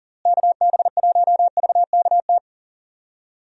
Morse code, heard as a keyed tone, about 38 wpm, 700 Hz. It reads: KB1VKT
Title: USA Call Signs - 6 Characters - Repeated 3x - 40wpm